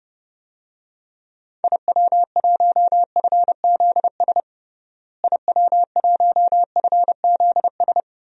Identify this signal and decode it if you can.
Morse code, heard as a keyed tone, about 31 wpm, 700 Hz. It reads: IW1FZS IW1FZS
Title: DXCC-100 Call Signs - 6 Characters - Repeated 2x - 30wpm